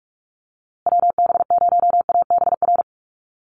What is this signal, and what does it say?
Morse code, heard as a keyed tone, about 44 wpm, 700 Hz. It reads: WB0ABR
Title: USA Call Signs Repeated 3x 45wpm